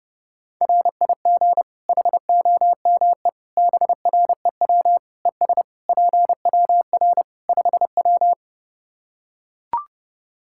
Morse code, heard as a keyed tone, about 32 wpm, 700 Hz. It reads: RIG HOME BREW ES PWR 5W
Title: K1USN SST QSO Practice 30wpm